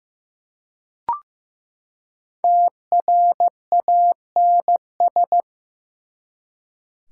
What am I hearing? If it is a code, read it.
Morse code, heard as a keyed tone, about 15 wpm, 700 Hz. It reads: TRANS
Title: Prefixes and Suffixes 15wpm